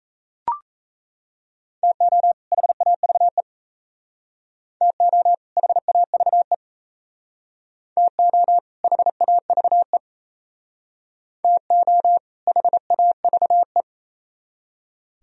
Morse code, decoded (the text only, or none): TO HAVE TO HAVE TO HAVE TO HAVE